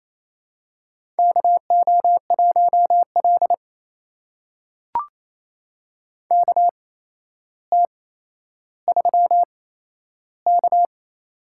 Morse code, heard as a keyed tone, about 28 wpm, spaced as for 12 wpm, 700 Hz. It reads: KO1L KT3K
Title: USA Call Signs - 4 Characters - 8x Character Spacing - 28wpm